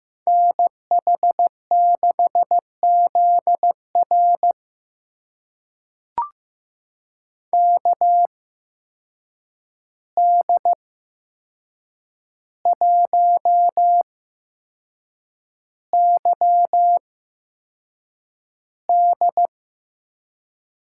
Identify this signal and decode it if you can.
Morse code, heard as a keyed tone, about 15 wpm, spaced as for 6 wpm, 700 Hz. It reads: NH6ZR KD1YD
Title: USA Call Signs - 5 Characters - 8x Character Spacing - 15wpm